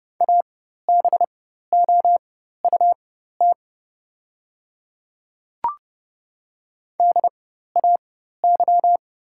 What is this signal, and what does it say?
Morse code, heard as a keyed tone, about 30 wpm, spaced as for 14 wpm, 700 Hz. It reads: ABOUT DAY
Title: 4x Character Spacing - Top 200 Words - 30wpm